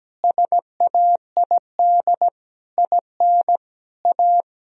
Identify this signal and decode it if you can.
Morse code, heard as a keyed tone, about 17 wpm, 700 Hz. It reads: SAID IN A
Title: Sentences from Top 1000 Words - Review Part 1 - 17wpm